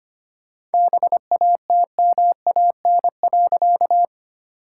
Morse code, heard as a keyed tone, about 25 wpm, 700 Hz. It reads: BATMAN.
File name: practice / 25wpm / All-Age Appropriate Jokes - 25wpm